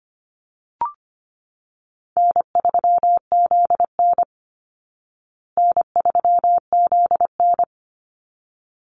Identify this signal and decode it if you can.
Morse code, heard as a keyed tone, about 25 wpm, 700 Hz. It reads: N3ZN N3ZN
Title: Easy Call Signs Repeated 2x - 25wpm